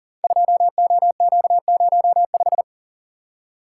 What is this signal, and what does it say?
Morse code, heard as a keyed tone, about 38 wpm, 700 Hz. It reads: 2OQ05
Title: Code Groups - 5 Letter:Numbers - 40wpm